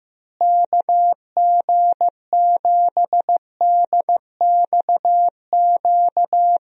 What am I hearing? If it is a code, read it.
Morse code, heard as a keyed tone, about 15 wpm, 700 Hz. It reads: KG7DXQ
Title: USA Call Signs - 6 Characters - Repeated 2x - 15wpm